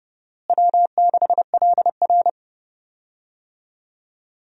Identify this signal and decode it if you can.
Morse code, heard as a keyed tone, about 30 wpm, 700 Hz. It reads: W6LR
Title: USA Call Signs - 4 Characters - Repeated 3x - 30wpm